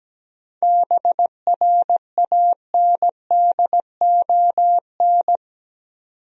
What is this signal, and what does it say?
Morse code, heard as a keyed tone, about 17 wpm, 700 Hz. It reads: BRANDON
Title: USA Common Names - 17wpm